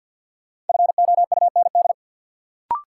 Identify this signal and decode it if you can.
Morse code, heard as a keyed tone, about 53 wpm, 700 Hz. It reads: FOUND
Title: Sentences from Top 300 Words - New Word Drill - 50wpm